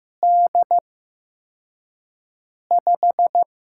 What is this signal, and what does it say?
Morse code, heard as a keyed tone, about 15 wpm, spaced as for 4 wpm, 700 Hz. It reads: D5
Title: USA Call Signs - 6 Characters - 8x Character Spacing - 15wpm